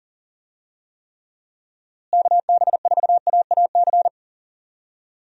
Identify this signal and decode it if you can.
Morse code, heard as a keyed tone, about 37 wpm, 700 Hz. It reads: KB4AAC
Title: USA Call Signs - 6 Characters - Repeated 2x - 40wpm